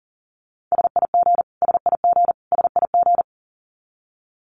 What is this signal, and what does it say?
Morse code, heard as a keyed tone, about 40 wpm, 700 Hz. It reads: SIG SIG SIG
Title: Abbreviations Repeated 3x - 40wpm